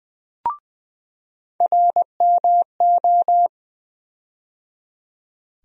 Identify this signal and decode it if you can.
Morse code, heard as a keyed tone, about 20 wpm, 700 Hz. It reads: RMO
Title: USA Call Sign Postfixes - 3 Letters 20wpm